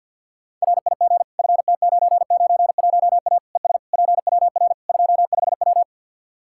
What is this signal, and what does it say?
Morse code, heard as a keyed tone, about 50 wpm, 700 Hz. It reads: RIG FT991A ES PWR 25W
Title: Practice QSO - Exchange 2 - Info - 50wpm